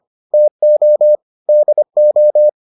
Morse code, heard as a keyed tone, about 25 wpm, 600 Hz. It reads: TO DO